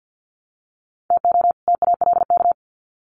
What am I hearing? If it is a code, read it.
Morse code, heard as a keyed tone, about 50 wpm, 700 Hz. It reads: TO TALK
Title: Sets of 2 Words 50wpm